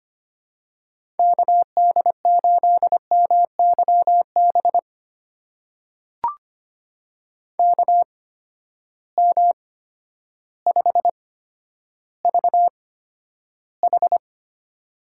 Morse code, heard as a keyed tone, about 25 wpm, spaced as for 11 wpm, 700 Hz. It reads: KD8MYB KM5VH
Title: USA Call Signs - 8x Character Spacing - 25wpm